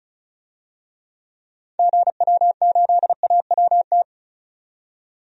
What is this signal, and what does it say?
Morse code, heard as a keyed tone, about 34 wpm, 700 Hz. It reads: GW8AWT
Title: DXCC-100 Call Signs - 6 Characters - 35wpm